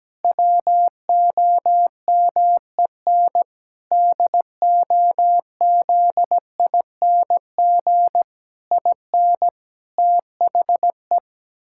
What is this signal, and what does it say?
Morse code, heard as a keyed tone, about 17 wpm, 700 Hz. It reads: WOMEN DOZING IN THE
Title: Pangram Challenge v2 - 17wpm